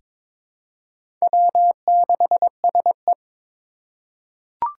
Morse code, heard as a keyed tone, about 22 wpm, 700 Hz. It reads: W6SE
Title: USA Call Signs - 4 Characters - 2x Character Spacing - 22wpm